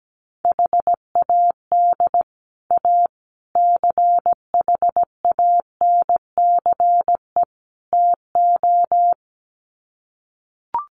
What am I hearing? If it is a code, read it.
Morse code, heard as a keyed tone, about 17 wpm, 700 Hz. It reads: HAD A CHANCE TO
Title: Sets of 4 Words 17wpm